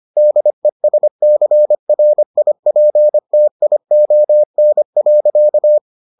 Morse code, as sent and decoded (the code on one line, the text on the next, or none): -.. ... -.-. .-. .. .--. - .. --- -. .-.-.-
DESCRIPTION.